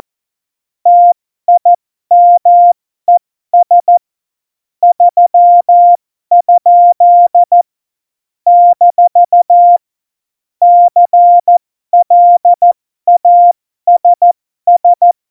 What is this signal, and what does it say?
Morse code, heard as a keyed tone, about 14 wpm, spaced as for 12 wpm, 700 Hz. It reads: TIMES 3? - CLASS
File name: Morsekurs / Witze